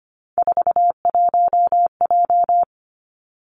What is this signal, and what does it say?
Morse code, heard as a keyed tone, about 25 wpm, 700 Hz. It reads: 41J
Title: Three Challenging Letter-Number Combinations 25wpm